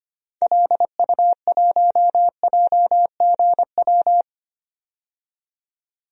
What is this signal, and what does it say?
Morse code, heard as a keyed tone, about 25 wpm, 700 Hz. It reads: LU1JGW
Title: DXCC-100 Call Signs - 6 Characters - 25wpm